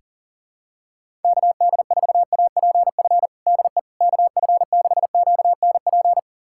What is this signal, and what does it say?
Morse code, heard as a keyed tone, about 40 wpm, 700 Hz. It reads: KD4APF DE KF6QNP
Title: USA Callsign DE Callsign - 6 Characters - 40wpm